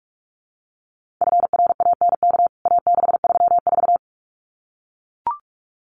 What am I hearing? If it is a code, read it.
Morse code, heard as a keyed tone, about 44 wpm, 700 Hz. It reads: FRANK A634
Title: CWT Exchange 45wpm